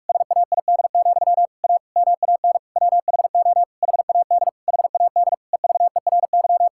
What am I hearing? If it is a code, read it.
Morse code, heard as a keyed tone, about 46 wpm, 700 Hz. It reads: SAID, A MAN WHO HAD HAD EVERY
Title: Sentences from Top 1000 Words - Review Part 6 - 45wpm